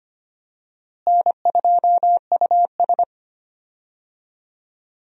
Morse code, heard as a keyed tone, about 25 wpm, 700 Hz. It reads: N2US
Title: USA Call Signs - 4 Characters - Repeated 3x - 25wpm